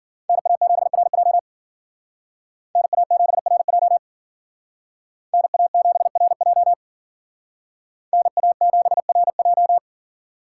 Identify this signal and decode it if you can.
Morse code, heard as a keyed tone, about 48 wpm, 700 Hz. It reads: NA7RJ NA7RJ NA7RJ NA7RJ
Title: Speed Racing - USA 2x2 Call Signs 40wpm